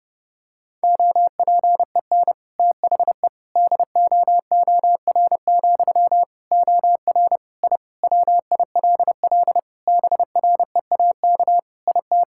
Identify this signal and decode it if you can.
Morse code, heard as a keyed tone, about 30 wpm, 700 Hz. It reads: OPEN THE DOOR, OR I WILL BREAK IT
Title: Sentences from Top 1000 Words - Review Part 6 - 30wpm